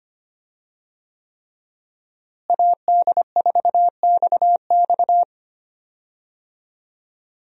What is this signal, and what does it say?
Morse code, heard as a keyed tone, about 25 wpm, 700 Hz. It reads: AD4XX
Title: USA Call Signs - 5 Characters - Repeated 3x - 25wpm